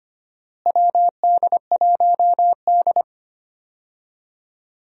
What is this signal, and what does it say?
Morse code, heard as a keyed tone, about 25 wpm, 700 Hz. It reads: WD1D
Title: USA Call Signs - 4 Characters - Repeated 2x - 25wpm